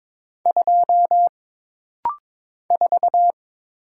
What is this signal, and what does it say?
Morse code, heard as a keyed tone, about 22 wpm, 700 Hz. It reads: 2 4
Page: This rapid-fire timing is geared to push your ICR (Instant Character Recognition) skills.